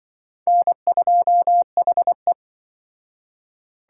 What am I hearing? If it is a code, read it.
Morse code, heard as a keyed tone, about 25 wpm, 700 Hz. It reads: N2HE
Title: Speed Racing - USA 1x2 Call Signs 20wpm